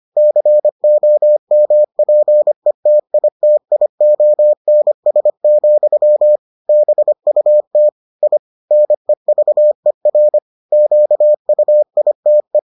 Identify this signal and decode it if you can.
Morse code, heard as a keyed tone, about 25 wpm, 600 Hz. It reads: COMPETITIONS, BUT I NEVER QUITE